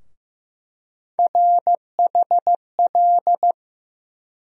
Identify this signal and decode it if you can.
Morse code, heard as a keyed tone, about 15 wpm, 700 Hz. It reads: RHL
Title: USA Call Sign Postfixes - 3 Letters 15wpm